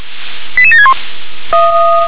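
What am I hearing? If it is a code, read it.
Five-tone selective call (ZVEI), heard as quick runs of five tones, tones of about 70 ms. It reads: 88761
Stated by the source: ZVEI five-tone selective call 88722, received by SDR.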